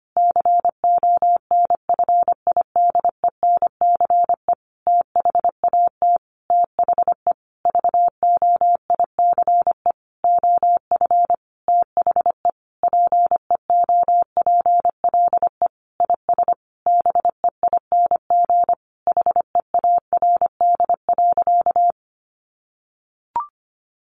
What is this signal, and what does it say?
Morse code, heard as a keyed tone, about 25 wpm, 700 Hz. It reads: CONFIDENCE THAT THE VOICE OF THE PEOPLE IS BEING HEARD.
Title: Sentences from Top 1500 Words 25wpm